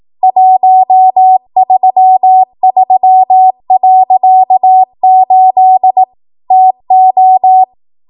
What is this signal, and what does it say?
Morse code, heard as a keyed tone, about 18 wpm, 750 Hz. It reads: 133.8 TO
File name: Archive / 18 WPM / 18WPM